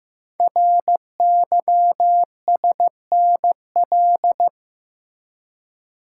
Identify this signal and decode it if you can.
Morse code, heard as a keyed tone, about 15 wpm, 700 Hz. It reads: RYSNL
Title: Code Groups - 5 Letters - 15wpm